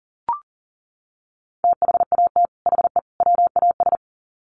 Morse code, heard as a keyed tone, about 40 wpm, 700 Hz. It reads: THAT HE WAS
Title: Sets of 3 Words 40wpm